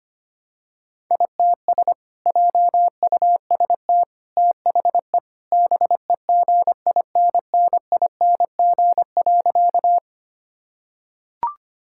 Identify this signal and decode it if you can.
Morse code, heard as a keyed tone, about 25 wpm, 700 Hz. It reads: ITS JUST THE BEGINNING.
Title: Sentences from Top 1000 Words - Review Part 5 - 25wpm